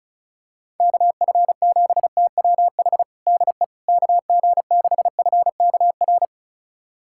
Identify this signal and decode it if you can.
Morse code, heard as a keyed tone, about 36 wpm, 700 Hz. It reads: KF7TWH DE KG6FKR
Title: USA Callsign DE Callsign - 6 Characters - 35wpm